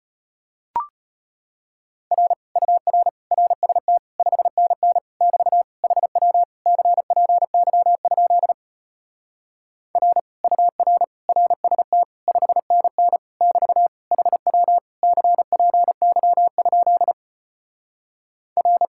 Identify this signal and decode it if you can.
Morse code, heard as a keyed tone, about 33 wpm, 700 Hz. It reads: R UR RST 5NN <BT> HW CPY? R UR RST 5NN <BT> HW CPY? R